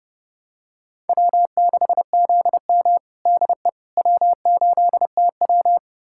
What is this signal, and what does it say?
Morse code, heard as a keyed tone, about 30 wpm, 700 Hz. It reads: W6ZM DE W8TW
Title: USA Callsign DE Callsign - 4 Characters - 30wpm